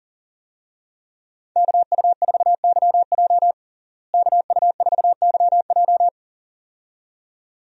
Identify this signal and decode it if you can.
Morse code, heard as a keyed tone, about 40 wpm, 700 Hz. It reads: KU4YJ KU4YJ
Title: USA Call Signs - 5 Characters - Repeated 2x - 40wpm